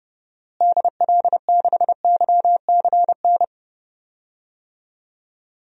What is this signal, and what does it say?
Morse code, heard as a keyed tone, about 30 wpm, 700 Hz. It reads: DL6YCN
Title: DXCC-100 Call Signs - 6 Characters - Repeated 3x - 30wpm